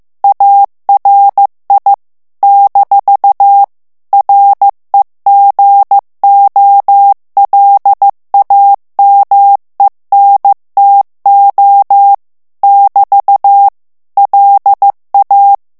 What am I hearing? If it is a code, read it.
Morse code, heard as a keyed tone, about 15 wpm, 800 Hz. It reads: ARI - REGOLAMENTO <BT> LA
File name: morse mp3 / ITALIEN texte clair 2 vitesse 14 mots minute